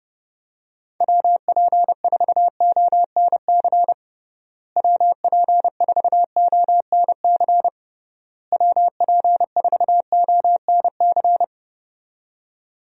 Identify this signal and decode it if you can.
Morse code, heard as a keyed tone, about 29 wpm, 700 Hz. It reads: WP4ONC WP4ONC WP4ONC
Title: USA Call Signs - 6 Characters - Repeated 3x - 30wpm